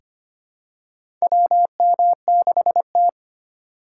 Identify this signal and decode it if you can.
Morse code, heard as a keyed tone, about 25 wpm, 700 Hz. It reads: WM6T
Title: USA Call Signs - 4 Characters - 6x Character Spacing - 25wpm